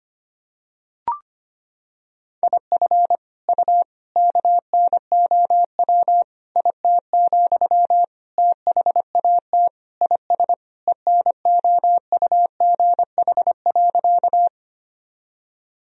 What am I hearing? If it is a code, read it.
Morse code, heard as a keyed tone, about 25 wpm, 700 Hz. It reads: IF U KNOW IT, THAT IS ENOUGH.
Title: Sentences from Top 400 Words and Abbreviations 25wpm